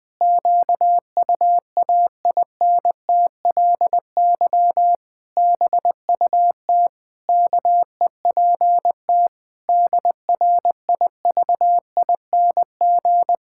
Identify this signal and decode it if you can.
Morse code, heard as a keyed tone, about 20 wpm, 700 Hz. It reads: QUAINTLY BUT KEPT DRIVING
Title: Pangram Challenge v2 - 20wpm